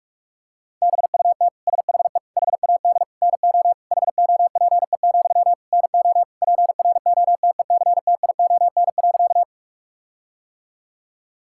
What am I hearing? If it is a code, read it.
Morse code, heard as a keyed tone, about 46 wpm, 700 Hz. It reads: BUT SHE HAD NO HOPE, NO PROTECTION.
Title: Sentences from Top 1600 Words 45wpm